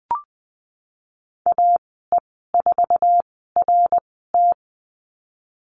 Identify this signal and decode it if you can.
Morse code, heard as a keyed tone, about 20 wpm, 700 Hz. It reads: A E 4 R T
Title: 055 Lesson 10g - Letter r - Call Signs 20wpm